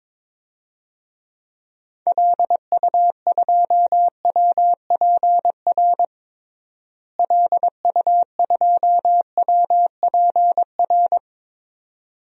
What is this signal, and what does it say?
Morse code, heard as a keyed tone, about 22 wpm, 700 Hz. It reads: LU2WPR LU2WPR
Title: DXCC-100 Call Signs - 6 Characters - Repeated 2x - 22wpm